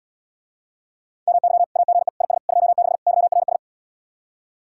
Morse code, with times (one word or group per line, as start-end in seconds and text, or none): 1.3-1.6 MY
1.8-2.1 AGE
2.2-2.4 IS
2.5-3.0 26
3.1-3.6 YRS